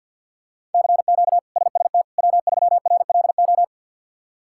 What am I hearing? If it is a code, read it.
Morse code, heard as a keyed tone, about 47 wpm, 700 Hz. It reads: CQ SST W3RLO
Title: K1USN SST QSO Practice - 50wpm